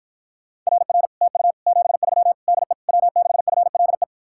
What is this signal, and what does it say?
Morse code, heard as a keyed tone, about 52 wpm, 700 Hz. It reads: RR TU 73 DE W6FLE